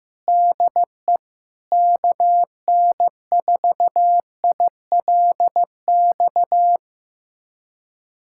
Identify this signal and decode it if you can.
Morse code, heard as a keyed tone, about 15 wpm, 700 Hz. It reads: DE KN4ILX
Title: USA Callsign DE Callsign - 6 Characters - 15wpm